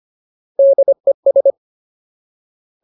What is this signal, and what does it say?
Morse code, heard as a keyed tone, about 25 wpm, 550 Hz. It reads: DES
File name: Ultimate Word List / 25WPM Ultimate